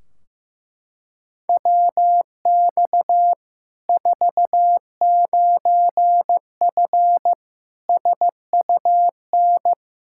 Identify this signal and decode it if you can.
Morse code, heard as a keyed tone, about 16 wpm, 700 Hz. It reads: WX 49F SUN
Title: Practice QSO - Exchange 2 - Info - 15wpm